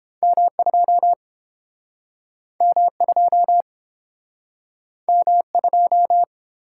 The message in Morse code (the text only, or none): M2 M2 M2